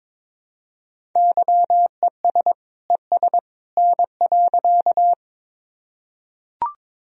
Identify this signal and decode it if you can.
Morse code, heard as a keyed tone, about 22 wpm, 700 Hz. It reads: YES ES N.
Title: Sentences from Top 500 Words and Abbreviations 22wpm